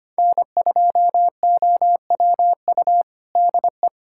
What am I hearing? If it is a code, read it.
Morse code, heard as a keyed tone, about 25 wpm, 700 Hz. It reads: N2OWU DE
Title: USA Callsign DE Callsign - 5 Characters - 25wpm